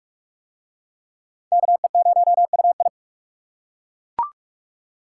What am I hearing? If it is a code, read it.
Morse code, heard as a keyed tone, about 38 wpm, 700 Hz. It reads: KE0UI